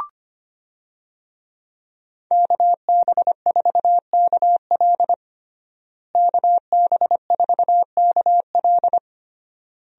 Morse code, heard as a keyed tone, about 25 wpm, 700 Hz. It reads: KB4KL KB4KL
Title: USA Call Signs - 5 Characters - Repeated 3x - 25wpm